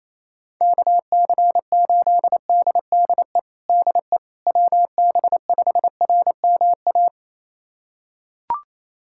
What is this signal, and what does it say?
Morse code, heard as a keyed tone, about 28 wpm, 700 Hz. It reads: KC8DDE DE WB5RMA
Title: USA Callsign DE Callsign - 6 Characters - 28wpm